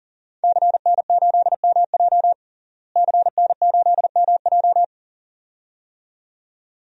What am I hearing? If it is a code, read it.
Morse code, heard as a keyed tone, about 37 wpm, 700 Hz. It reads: CN8MJ CN8MJ
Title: DXCC-100 Call Signs - 5 Characters - Repeated 2x - 40wpm